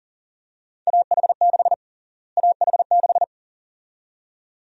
Morse code, heard as a keyed tone, about 40 wpm, 700 Hz. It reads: AH6 AH6
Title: USA Call Sign Prefixes - Repeated 2x - 40wpm